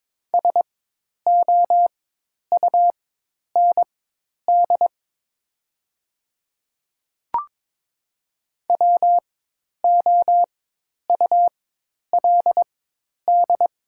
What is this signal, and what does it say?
Morse code, heard as a keyed tone, about 22 wpm, spaced as for 10 wpm, 700 Hz. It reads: SOUND WOULD
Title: 4x Character Spacing - Top 100 Words - 22wpm